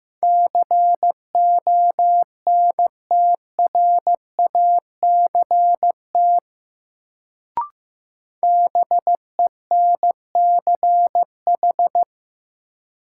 Morse code, heard as a keyed tone, about 15 wpm, 700 Hz. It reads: CONTRACT BENCH
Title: Sentences from Top 1400 Words - New Word Drill - 15wpm